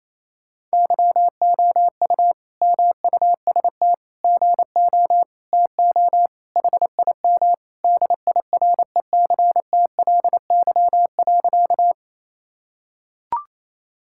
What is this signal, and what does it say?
Morse code, heard as a keyed tone, about 28 wpm, 700 Hz. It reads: YOU MUST GO TO HIM DIRECTLY.
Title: Sentences from Top 1000 Words - Review Part 5 - 28wpm